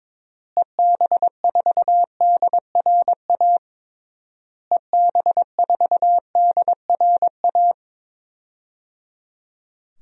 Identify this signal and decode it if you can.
Morse code, heard as a keyed tone, about 22 wpm, 700 Hz. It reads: EB4DRA EB4DRA
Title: DXCC-100 Call Signs - 6 Characters - Repeated 3x - 22wpm